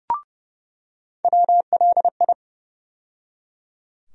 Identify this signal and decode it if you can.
Morse code, heard as a keyed tone, about 30 wpm, 700 Hz. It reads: WLI
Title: USA Call Sign Postfixes - 3 Letters 30wpm